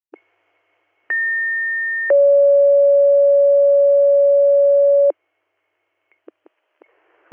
Two-tone page, heard as a one-second tone, then a three-second tone, about 1800 then 560 Hz. It is heard from 1.1 to 5.1 s.